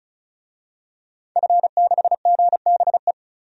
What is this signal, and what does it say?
Morse code, heard as a keyed tone, about 36 wpm, 700 Hz. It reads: F6GBE
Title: DXCC-100 Call Signs - 5 Characters - 35wpm